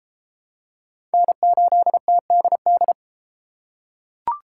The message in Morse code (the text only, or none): N8TDD